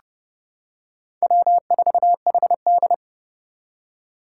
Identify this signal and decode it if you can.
Morse code, heard as a keyed tone, about 30 wpm, 700 Hz. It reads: W4HD
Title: Four Challenging Letter-Number Combinations 30wpm